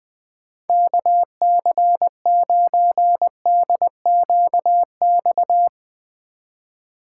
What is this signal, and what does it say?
Morse code, heard as a keyed tone, about 21 wpm, 700 Hz. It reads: KC9DQX
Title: Speed Racing - USA 2x3 Call Signs 20wpm